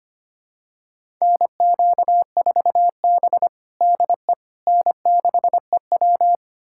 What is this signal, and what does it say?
Morse code, heard as a keyed tone, about 25 wpm, 700 Hz. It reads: NQ4B DE N6EW
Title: USA Callsign DE Callsign - 4 Characters - 25wpm